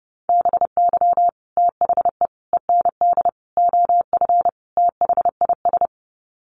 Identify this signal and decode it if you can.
Morse code, heard as a keyed tone, about 30 wpm, 700 Hz. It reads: BY THE END OF THIS
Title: Sets of 5 Words Encore 30wpm